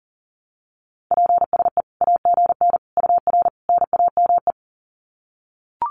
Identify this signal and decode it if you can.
Morse code, heard as a keyed tone, about 40 wpm, 700 Hz. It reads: PSE AGN UR NAME
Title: Common QSO Phrases 40wpm